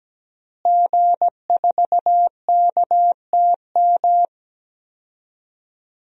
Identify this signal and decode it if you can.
Morse code, heard as a keyed tone, about 17 wpm, 700 Hz. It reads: G4KTM
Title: DXCC-100 Call Signs - 5 Characters - Repeated 3x - 17wpm